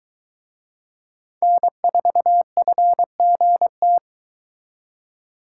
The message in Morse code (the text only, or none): N4FGT